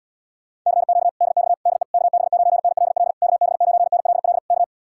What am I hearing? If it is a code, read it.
Morse code, heard as a keyed tone, about 78 wpm, 700 Hz. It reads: CQ DX DE KB9NLL KB9NLL K